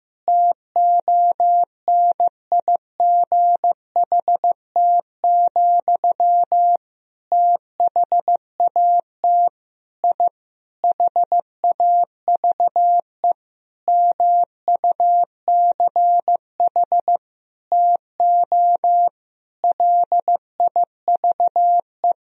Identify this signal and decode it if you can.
Morse code, heard as a keyed tone, about 16 wpm, 700 Hz. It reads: TONIGHT, THAT I HAVE MUCH TO LIVE
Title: Sentences from Top 1000 Words - Review Part 3 - 15wpm